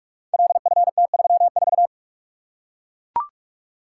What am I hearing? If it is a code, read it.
Morse code, heard as a keyed tone, about 44 wpm, 700 Hz. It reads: LFT34
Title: Code Groups - 5 Letter:Numbers - 45wpm